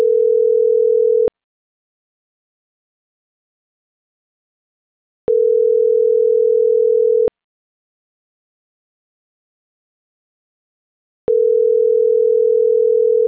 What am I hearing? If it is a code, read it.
Ringback tone (North America), 440 + 480 Hz.